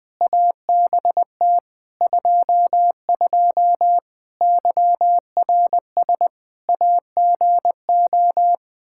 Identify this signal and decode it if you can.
Morse code, heard as a keyed tone, about 20 wpm, 700 Hz. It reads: ABT 22 YRS AGO
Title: Practice QSO - Exchange 3 - More Info - 20wpm